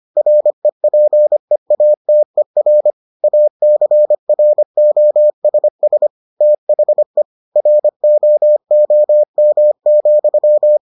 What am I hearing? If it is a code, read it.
Morse code, heard as a keyed tone, about 25 wpm, 600 Hz. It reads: REPEATER ACROSS THE ROOM,